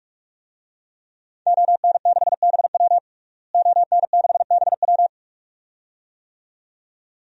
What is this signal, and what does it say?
Morse code, heard as a keyed tone, about 41 wpm, 700 Hz. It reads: ON6BW ON6BW
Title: DXCC-100 Call Signs - 5 Characters - Repeated 2x - 45wpm